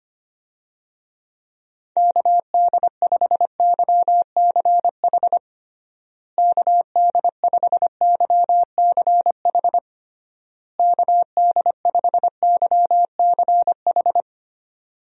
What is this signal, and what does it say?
Morse code, heard as a keyed tone, about 25 wpm, 700 Hz. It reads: KD5YCH KD5YCH KD5YCH
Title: USA Call Signs - 6 Characters - Repeated 3x - 25wpm